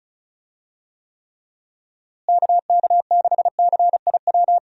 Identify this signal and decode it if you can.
Morse code, heard as a keyed tone, about 34 wpm, 700 Hz. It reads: KK6CIW
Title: USA Call Signs - 6 Characters - Repeated 2x - 35wpm